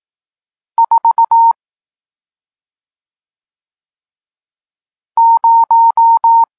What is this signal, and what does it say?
Morse code, heard as a keyed tone, about 18 wpm, 920 Hz. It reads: 4 0
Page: Tekst 40 – Nauka znaków przestankowych